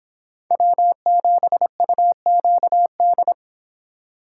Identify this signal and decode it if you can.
Morse code, heard as a keyed tone, about 24 wpm, 700 Hz. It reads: W7UQD